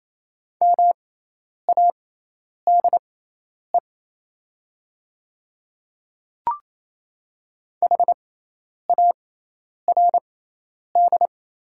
Morse code, heard as a keyed tone, about 27 wpm, spaced as for 10 wpm, 700 Hz. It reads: MADE HARD
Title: 6x Character Spacing - Top 500 Words - 28wpm